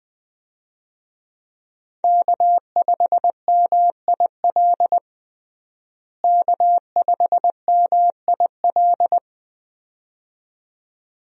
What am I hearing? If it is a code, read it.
Morse code, heard as a keyed tone, about 20 wpm, 700 Hz. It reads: K5MIL K5MIL
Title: USA Call Signs - 5 Characters - Repeated 2x - 20wpm